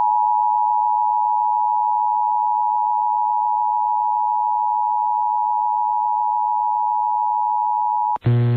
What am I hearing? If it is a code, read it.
Emergency Alert System (SAME) tones (the 853 and 960 Hz attention tone). It is heard from 0.4 to 8.1 s.